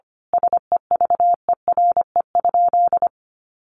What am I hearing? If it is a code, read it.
Morse code, heard as a keyed tone, about 25 wpm, 700 Hz. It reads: SEVERE?
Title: Sentences from Top 1400 Words 25wpm